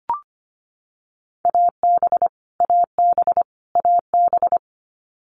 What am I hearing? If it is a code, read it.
Morse code, heard as a keyed tone, about 25 wpm, 700 Hz. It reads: AB AB AB
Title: Canadian Provinces and Territories - Repeated 3x - 25wpm